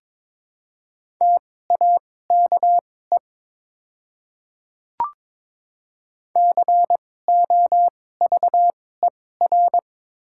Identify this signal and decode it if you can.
Morse code, heard as a keyed tone, about 22 wpm, spaced as for 16 wpm, 700 Hz. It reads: TAKE COVER
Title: 2x Character Spacing - Top 300 Words - 22wpm